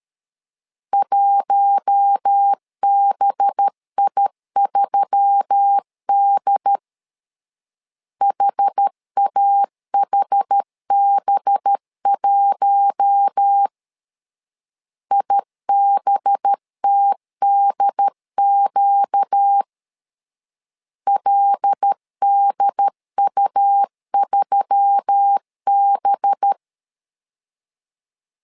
Morse code, heard as a keyed tone, about 13 wpm, 800 Hz. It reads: 1BI3D HAHB1 IBTDQ LDU3B
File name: LEZIONI / 5 LBVC / 5 LB 12wpm 1906